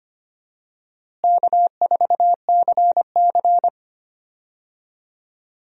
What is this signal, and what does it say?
Morse code, heard as a keyed tone, about 25 wpm, 700 Hz. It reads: K4CC
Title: USA Call Signs - 4 Characters - Repeated 3x - 25wpm